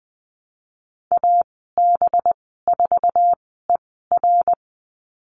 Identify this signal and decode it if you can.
Morse code, heard as a keyed tone, about 20 wpm, spaced as for 14 wpm, 700 Hz. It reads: AB4ER
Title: 245 Lesson 39g - Letter z - Call Signs 20wpm